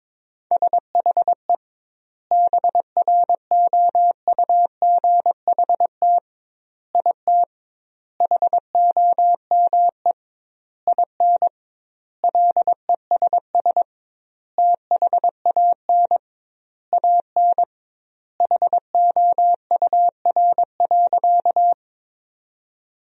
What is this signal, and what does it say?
Morse code, heard as a keyed tone, about 21 wpm, 700 Hz. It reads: SHE BROUGHT IT HOME IN LESS THAN AN HOUR.
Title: 2x Word-Spacing - Sentences from Top 500 Words - 22wpm